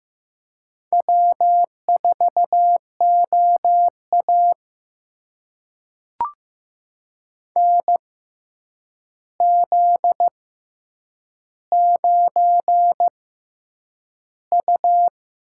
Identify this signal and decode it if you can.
Morse code, heard as a keyed tone, about 15 wpm, spaced as for 8 wpm, 700 Hz. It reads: W4OA NZ9U
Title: USA Call Signs - 4 Characters - 6x Character Spacing - 15wpm